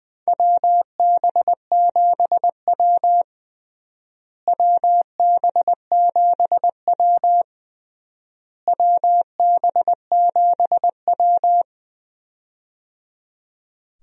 Morse code, heard as a keyed tone, about 20 wpm, 700 Hz. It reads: WB7W WB7W WB7W
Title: USA Call Signs - 4 Characters - Repeated 3x - 20wpm